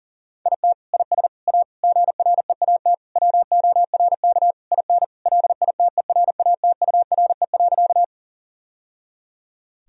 Morse code, heard as a keyed tone, about 40 wpm, 700 Hz. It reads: IT IS A GREAT WORK IN LITERATURE.
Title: Sentences from Top 1000 Words - Review Part 3 - 40wpm